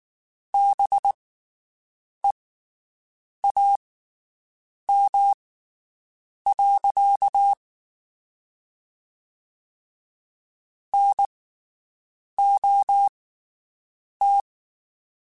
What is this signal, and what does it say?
Morse code, heard as a keyed tone, about 19 wpm, spaced as for 7 wpm, 800 Hz. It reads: BEAM. NOT
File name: (spaced) / QSOs Spaced-mp3s / WD0BGP 1 @18 sp